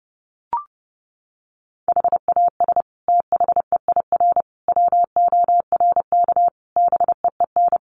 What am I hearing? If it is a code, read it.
Morse code, heard as a keyed tone, about 28 wpm, 700 Hz. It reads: HAS THEIR WORK BEEN
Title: Sentences from Top 1500 Words 30wpm